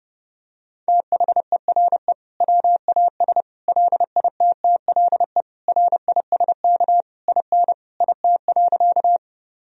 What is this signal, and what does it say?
Morse code, heard as a keyed tone, about 30 wpm, 700 Hz. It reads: THERE WAS LITTLE RISK IN IT.
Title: Sentences from Top 1000 Words - Review Part 4 - 30wpm